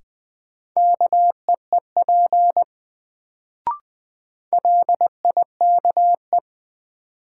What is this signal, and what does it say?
Morse code, heard as a keyed tone, about 20 wpm, 700 Hz. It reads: KEEP LIKE
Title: Sets of 2 Words - New Word Drill - 20wpm